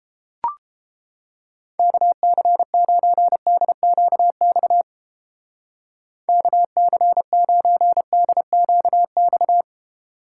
Morse code, heard as a keyed tone, about 29 wpm, 700 Hz. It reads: KC9DQX KC9DQX